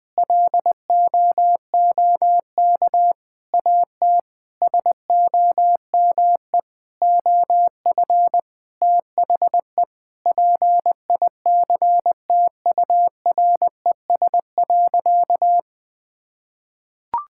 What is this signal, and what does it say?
Morse code, heard as a keyed tone, about 20 wpm, 700 Hz. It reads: LOOK AT SOME OF THE PICTURES.
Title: Sentences with 8-Word Phrases - 20wpm